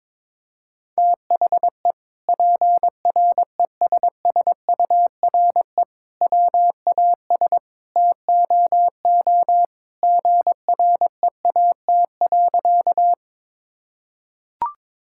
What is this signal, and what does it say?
Morse code, heard as a keyed tone, about 22 wpm, 700 Hz. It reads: THE PRESSURE WAS TOO GREAT.
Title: Sentences from Top 1000 Words - Review Part 6 - 22wpm